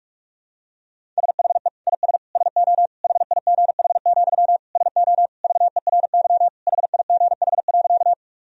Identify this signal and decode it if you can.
Morse code, heard as a keyed tone, about 43 wpm, 700 Hz. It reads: SHE IS SO HIGH, SO VERY HIGH.
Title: Sentences from Top 1000 Words - Review Part 1 - 45wpm